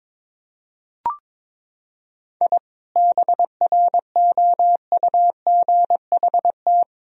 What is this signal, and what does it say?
Morse code, heard as a keyed tone, about 21 wpm, 700 Hz. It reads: I BROUGHT
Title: Sentences from Top 1000 Words - Review Part 4 - 22wpm